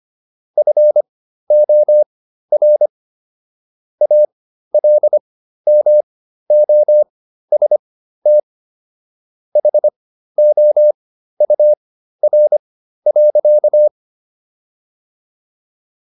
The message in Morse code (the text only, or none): FOR ALMOST HOUR.